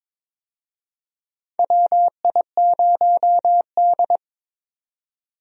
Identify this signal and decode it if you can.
Morse code, heard as a keyed tone, about 22 wpm, 700 Hz. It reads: WI0D
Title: USA Call Signs - 4 Characters - 22wpm